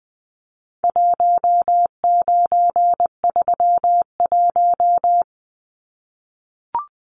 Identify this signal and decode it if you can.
Morse code, heard as a keyed tone, about 20 wpm, 700 Hz. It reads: 1931
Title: Numbers - 4-Digit 20wpm